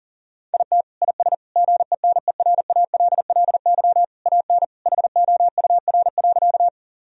Morse code, heard as a keyed tone, about 40 wpm, 700 Hz. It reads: IT IS GENERALLY AN HOUR.
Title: Sentences from Top 1000 Words - Review Part 4 - 40wpm